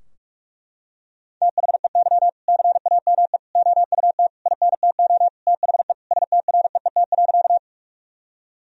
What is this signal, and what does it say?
Morse code, heard as a keyed tone, about 41 wpm, 700 Hz. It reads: THEY CAME OUT INTO THE STREET.
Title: Sentences from Top 1000 Words - Review Part 3 - 45wpm